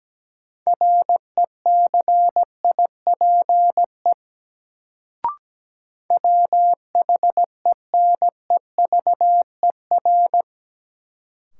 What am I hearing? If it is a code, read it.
Morse code, heard as a keyed tone, about 17 wpm, 700 Hz. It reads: RECIPE WHENEVER
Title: Sentences from Top 1200 Words - New Word Drill Alt - 17wpm